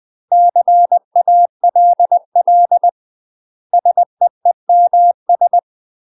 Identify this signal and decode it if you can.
Morse code, heard as a keyed tone, about 20 wpm, 700 Hz. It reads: CALL SEEMS
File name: UK MASTER QSOs TRAINING / 20WPM / UK MASTER QSOs 0030